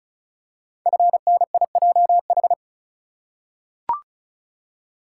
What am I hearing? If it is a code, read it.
Morse code, heard as a keyed tone, about 36 wpm, 690 Hz. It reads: FNIJH
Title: Code Groups - 5 Letters - 35wpm